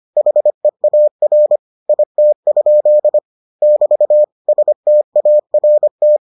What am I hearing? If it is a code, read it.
Morse code, heard as a keyed tone, about 25 wpm, 600 Hz. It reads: HEAR IT? <BT> START